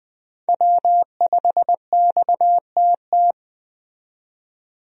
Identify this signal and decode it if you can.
Morse code, heard as a keyed tone, about 20 wpm, 700 Hz. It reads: W5XTT
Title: USA Call Signs - 5 Characters - 20wpm